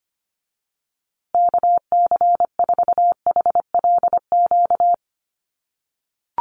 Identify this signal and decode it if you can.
Morse code, heard as a keyed tone, about 25 wpm, 700 Hz. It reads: KC4HLQ
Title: USA Call Signs Repeated 3x 25wpm